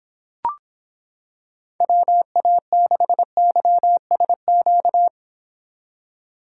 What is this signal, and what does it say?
Morse code, heard as a keyed tone, about 25 wpm, 700 Hz. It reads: WA6YSQ